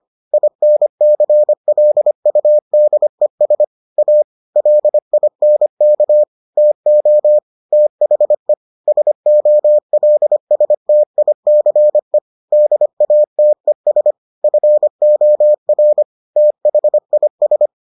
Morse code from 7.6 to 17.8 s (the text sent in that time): THE SOLSTICE DATES FOR THIS